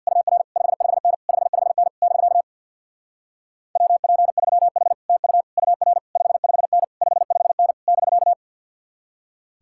Morse code, heard as a keyed tone, about 67 wpm, 700 Hz. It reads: UR 55N 55N <BK> WP3H TU UR 55N 55N <BK>